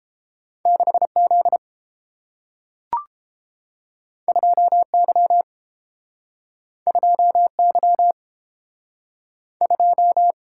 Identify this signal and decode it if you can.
Morse code, heard as a keyed tone, about 30 wpm, 700 Hz. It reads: 6Z 2Y 2Y 2